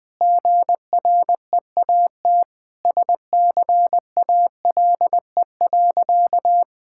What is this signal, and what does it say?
Morse code, heard as a keyed tone, about 21 wpm, 700 Hz. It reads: GREAT SCALE.
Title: Sentences from Top 1000 Words - Review Part 5 - 20wpm